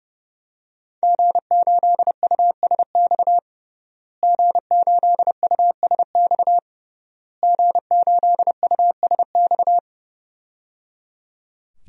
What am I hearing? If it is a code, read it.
Morse code, heard as a keyed tone, about 28 wpm, 700 Hz. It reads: G8USX G8USX G8USX
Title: DXCC-100 Call Signs - 5 Characters - Repeated 3x - 30wpm